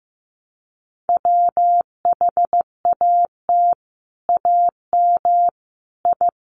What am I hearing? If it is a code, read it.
Morse code, heard as a keyed tone, about 15 wpm, 700 Hz. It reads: WHAT AM I
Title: Sets of 5 Words Encore2 15wpm